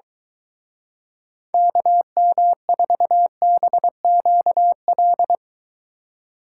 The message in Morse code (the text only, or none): KM4BQL